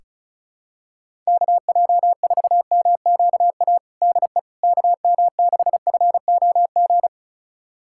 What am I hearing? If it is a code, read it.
Morse code, heard as a keyed tone, about 36 wpm, 700 Hz. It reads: KJ4MQA DE KM6FOG
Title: USA Callsign DE Callsign - 6 Characters - 35wpm